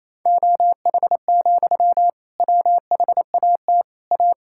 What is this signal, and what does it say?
Morse code, heard as a keyed tone, about 28 wpm, 700 Hz. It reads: OH, WHAT A
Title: Sentences from Top 1000 Words - Review Part 6 - 28wpm